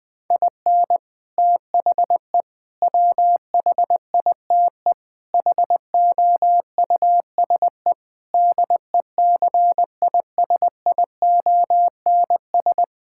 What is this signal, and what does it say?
Morse code, heard as a keyed tone, about 20 wpm, 700 Hz. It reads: IN THE WHITE HOUSE DECISIONS
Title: Sentences with 4-Word Phrases - 20wpm